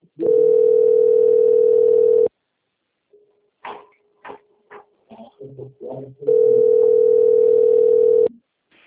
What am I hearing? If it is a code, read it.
Ringback tone (North America), 440 + 480 Hz.
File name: rg-1600-unknown- - - .3792